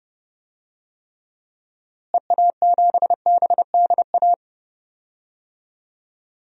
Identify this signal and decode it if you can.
Morse code, heard as a keyed tone, about 32 wpm, 700 Hz. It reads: EA7BDA
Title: DXCC-100 Call Signs - 6 Characters - Repeated 2x - 30wpm